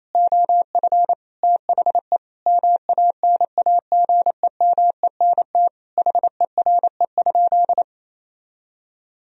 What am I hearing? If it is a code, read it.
Morse code, heard as a keyed tone, about 28 wpm, 700 Hz. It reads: OF THE MANAGEMENT HERE?
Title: Sentences from Top 1000 Words - Review Part 4 - 28wpm